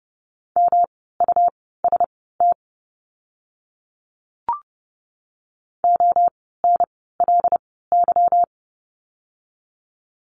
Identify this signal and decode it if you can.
Morse code, heard as a keyed tone, about 30 wpm, spaced as for 17 wpm, 700 Hz. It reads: MUST ONLY
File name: course / 30:10wpm / 237 Lesson 38f - Letter q - Words 30wpm